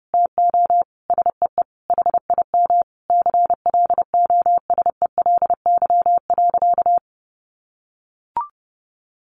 Morse code, heard as a keyed tone, about 30 wpm, 700 Hz. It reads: TO SEE HIM CLOSELY.
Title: Sentences from Top 1400 Words 30wpm